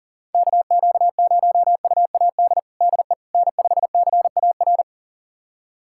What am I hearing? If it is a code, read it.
Morse code, heard as a keyed tone, about 40 wpm, 700 Hz. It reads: KQ0UAD DE N5CAR
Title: USA Callsign DE Callsign 40wpm